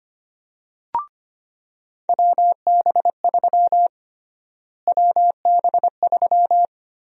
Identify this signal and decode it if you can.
Morse code, heard as a keyed tone, about 25 wpm, 700 Hz. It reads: WB3 WB3
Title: USA Call Sign Prefixes - Repeated 2x - 25wpm